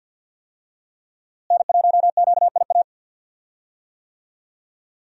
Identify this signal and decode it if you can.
Morse code, heard as a keyed tone, about 52 wpm, 700 Hz. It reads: N1QIA
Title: USA Call Signs - 5 Characters - Repeated 3x - 50wpm